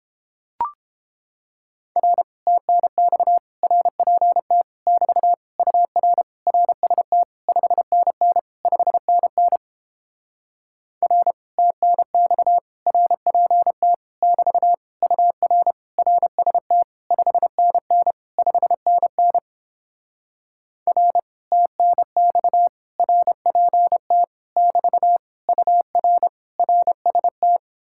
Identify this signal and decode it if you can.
Morse code, heard as a keyed tone, about 29 wpm, 700 Hz. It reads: R TNX RPT <BT> UR RST 5NN 5NN R TNX RPT <BT> UR RST 5NN 5NN R TNX RPT <BT> UR RST